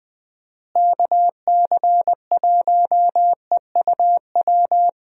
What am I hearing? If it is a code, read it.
Morse code, heard as a keyed tone, about 20 wpm, 700 Hz. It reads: KC1EUW
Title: USA Call Signs - 6 Characters - Repeated 3x - 20wpm